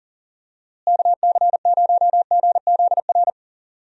Morse code, heard as a keyed tone, about 40 wpm, 700 Hz. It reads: KC0GZR
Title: USA Call Signs - 6 Characters - Repeated 2x - 40wpm